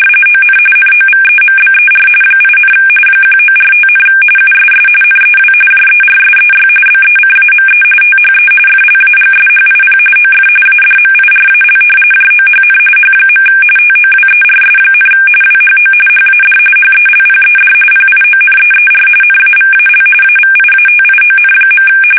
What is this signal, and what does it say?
RTTY, about 45.45 baud. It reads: ON NOVEMBER 8, 1945, USS NEW JERSEY BECAME THE FLAGSHIP OF ADMIRAL JOHN H. TOWERS, WHO HAD HELPED PIONEER NAVAL AVIATION.
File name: fsk850 / battleshipnj